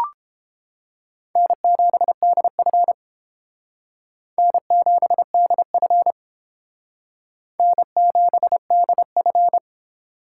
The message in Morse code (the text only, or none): N7DF N7DF N7DF